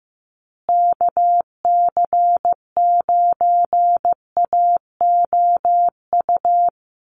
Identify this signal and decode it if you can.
Morse code, heard as a keyed tone, about 15 wpm, 700 Hz. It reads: KC9AOU
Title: USA Call Signs Repeated 3x 15wpm